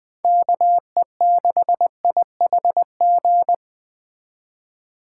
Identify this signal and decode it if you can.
Morse code, heard as a keyed tone, about 20 wpm, 700 Hz. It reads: KE6IHG
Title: Speed Racing - USA Call Signs 17wpm